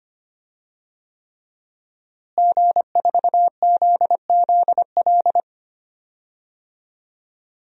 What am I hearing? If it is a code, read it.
Morse code, heard as a keyed tone, about 25 wpm, 700 Hz. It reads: G4ZZL
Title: DXCC-100 Call Signs - 5 Characters - Repeated 3x - 25wpm